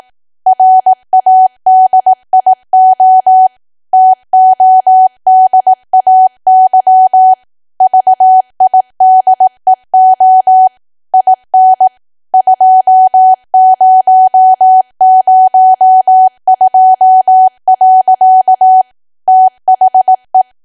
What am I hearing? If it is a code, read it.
Morse code, heard as a keyed tone, about 18 wpm, 750 Hz. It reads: RADIO TODAY VIDEO IN 2002. THE
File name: Archive / 18 WPM / 18WPM